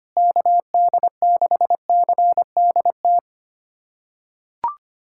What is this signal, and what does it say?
Morse code, heard as a keyed tone, about 25 wpm, 700 Hz. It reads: KD6CDT
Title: USA Callsign DE Callsign 25wpm